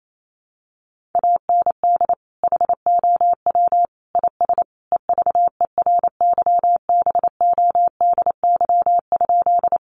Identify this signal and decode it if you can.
Morse code, heard as a keyed tone, about 28 wpm, 700 Hz. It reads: AND HOW IS EVERYBODY?
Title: Sentences from Top 1400 Words 28wpm